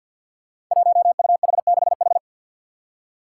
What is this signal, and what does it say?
Morse code, heard as a keyed tone, about 47 wpm, 700 Hz. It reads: 1UH6H
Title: Five Challenging Letter-Number Combinations 50wpm